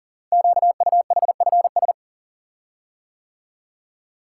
Morse code, heard as a keyed tone, about 38 wpm, 700 Hz. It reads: QUHFS
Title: Code Groups - 5 Letters - 40wpm